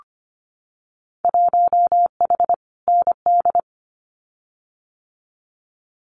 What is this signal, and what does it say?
Morse code, heard as a keyed tone, about 25 wpm, 700 Hz. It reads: 1H ND
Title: Winter Field Day Exchanges 25wpm